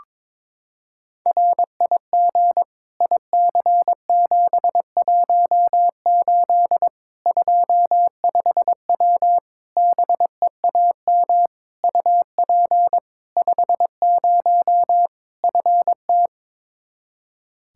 Morse code, heard as a keyed tone, about 22 wpm, 700 Hz. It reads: RIG IC718 25W BEAM UP 50 FT
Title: K1USN SST QSO Practice 22wpm